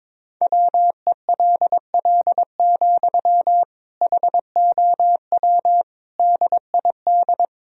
Morse code, heard as a keyed tone, about 22 wpm, 700 Hz. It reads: WELL, HOW DID
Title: Sentences from Top 1000 Words - Review Part 2 - 22wpm